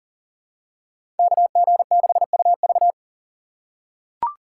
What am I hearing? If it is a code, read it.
Morse code, heard as a keyed tone, about 40 wpm, 700 Hz. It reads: KG6UV
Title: USA Call Signs - 5 Characters - 8x Character Spacing - 40wpm